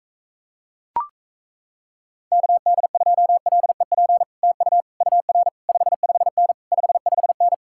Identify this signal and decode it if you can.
Morse code, heard as a keyed tone, about 37 wpm, 700 Hz. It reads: KD2LEP TU UR 55N 55N